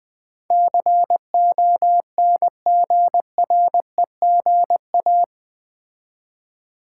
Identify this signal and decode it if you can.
Morse code, heard as a keyed tone, about 20 wpm, 700 Hz. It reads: CONGREGA
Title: Build A Word - 4 Letters At A Time - 12-Letter Words - 20wpm